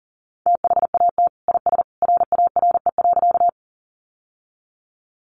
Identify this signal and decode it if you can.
Morse code, heard as a keyed tone, about 40 wpm, 700 Hz. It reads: THAT IS RARE.
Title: Sentences from Top 1100 Words 40wpm